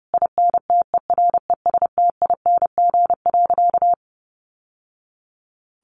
Morse code, heard as a keyed tone, about 30 wpm, 700 Hz. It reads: INTERESTING.
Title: Sentences from Top 1000 Words 30wpm